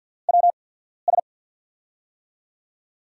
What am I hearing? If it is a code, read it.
Morse code, heard as a keyed tone, about 52 wpm, spaced as for 14 wpm, 700 Hz. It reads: WS
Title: USA Call Signs - 5 Characters - 8x Character Spacing - 50wpm